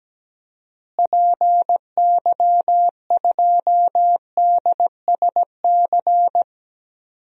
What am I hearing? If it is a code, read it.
Morse code, heard as a keyed tone, about 17 wpm, 700 Hz. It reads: PY2DSC
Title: DXCC-100 Call Signs - 6 Characters - Repeated 2x - 17wpm